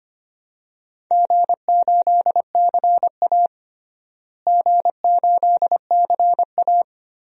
Morse code, heard as a keyed tone, about 25 wpm, 700 Hz. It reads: G8CA G8CA
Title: DXCC-100 Call Signs - 4 Characters - Repeated 2x - 25wpm